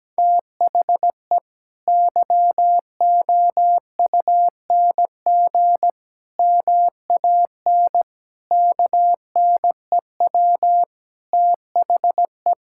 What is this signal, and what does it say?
Morse code, heard as a keyed tone, about 17 wpm, 700 Hz. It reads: THE YOUNG MAN KNEW THE
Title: Sentences from Top 1000 Words - Review Part 6 - 17wpm